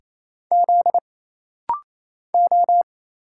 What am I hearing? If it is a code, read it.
Morse code, heard as a keyed tone, about 28 wpm, 700 Hz. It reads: Z O
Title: Single Letters - Rapid-Fire 28wpm